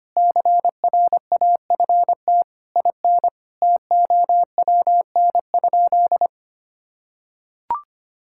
Morse code, heard as a keyed tone, about 25 wpm, 700 Hz. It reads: CRAFT IN TOWN?
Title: Sentences from Top 1000 Words - Review Part 6 - 25wpm